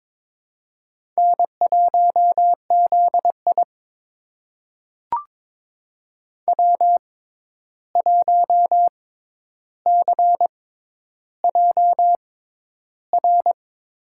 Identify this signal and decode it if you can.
Morse code, heard as a keyed tone, about 21 wpm, spaced as for 10 wpm, 700 Hz. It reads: N1ZI W1CJR
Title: USA Call Signs - 6x Character Spacing - 22wpm